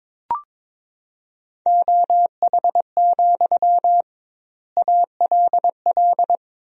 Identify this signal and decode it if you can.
Morse code, heard as a keyed tone, about 22 wpm, 700 Hz. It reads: OH, ALL
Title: 2x Word-Spacing - Sentences from Top 500 Words - 22wpm